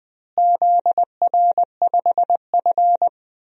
Morse code, heard as a keyed tone, about 20 wpm, 700 Hz. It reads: ZR5F
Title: DXCC-100 Call Signs - 4 Characters - Repeated 2x - 20wpm